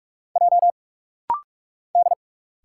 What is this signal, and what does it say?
Morse code, heard as a keyed tone, about 46 wpm, 700 Hz. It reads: J D
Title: Single Letter-Number Alt - Rapid-Fire 45wpm